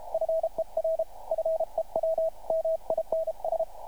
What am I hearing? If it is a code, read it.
Morse code, heard as a keyed tone, about 32 wpm, 650 Hz, with radio noise. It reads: FER FEW MINS
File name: archived / CRSnet